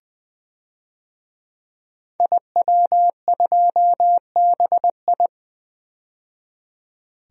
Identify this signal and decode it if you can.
Morse code, heard as a keyed tone, about 20 wpm, 700 Hz. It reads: IW2BI
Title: DXCC-100 Call Signs - 5 Characters - 20wpm